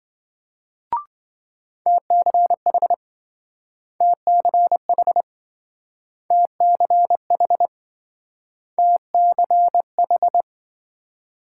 Morse code, heard as a keyed tone, about 27 wpm, 700 Hz. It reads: TCH TCH TCH TCH